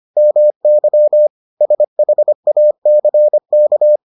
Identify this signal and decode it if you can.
Morse code, heard as a keyed tone, about 25 wpm, 600 Hz. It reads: MY SHACK